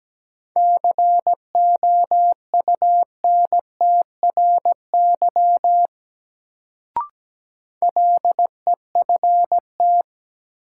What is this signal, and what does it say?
Morse code, heard as a keyed tone, about 17 wpm, 700 Hz. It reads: COUNTRY LEFT
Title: Sentences from Top 300 Words - New Word Drill - 17wpm